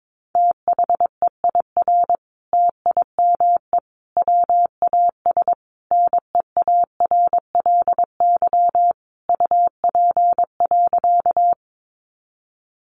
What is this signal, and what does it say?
Morse code, heard as a keyed tone, about 22 wpm, 700 Hz. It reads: THEIR TIME WAS NEARLY UP.
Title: Sentences from Top 1300 Words 22wpm